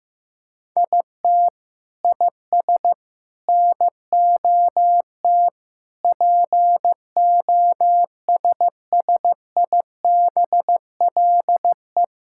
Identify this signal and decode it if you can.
Morse code, heard as a keyed tone, about 16 wpm, 700 Hz. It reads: IT IS NOT POSSIBLE
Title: Sentences from Top 1000 Words - Review Part 5 - 15wpm